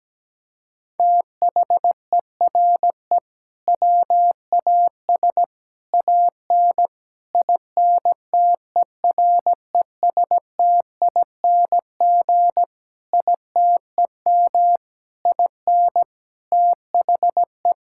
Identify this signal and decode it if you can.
Morse code, heard as a keyed tone, about 17 wpm, 700 Hz. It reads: THERE WAS AN INTERESTING ITEM IN THE
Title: Sentences from Top 1000 Words - Review Part 6 - 17wpm